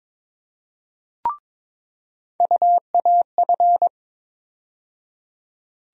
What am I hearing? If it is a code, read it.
Morse code, heard as a keyed tone, about 22 wpm, 700 Hz. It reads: UAF
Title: USA Call Sign Postfixes - 3 Letters 22wpm